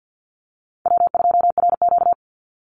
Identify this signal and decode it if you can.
Morse code, heard as a keyed tone, about 50 wpm, 700 Hz. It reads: W2RQ
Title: Call Signs - 50wpm